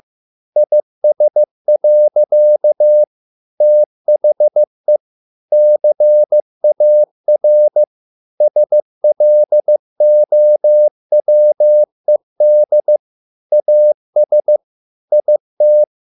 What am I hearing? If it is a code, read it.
Morse code, heard as a keyed tone, about 15 wpm, 600 Hz. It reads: IS. THE CAR SLOWED AS IT